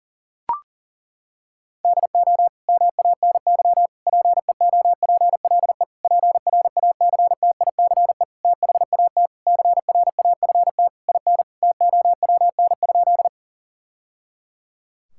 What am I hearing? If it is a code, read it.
Morse code, heard as a keyed tone, about 41 wpm, 700 Hz. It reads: DO MANY PEOPLE PRACTICE THAT CRAFT IN TOWN?
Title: Sentences from Top 1000 Words - Review Part 6 - 40wpm